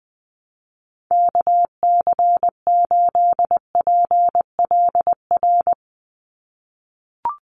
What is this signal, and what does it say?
Morse code, heard as a keyed tone, about 20 wpm, 700 Hz. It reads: KC8PLR
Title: USA Call Signs 20wpm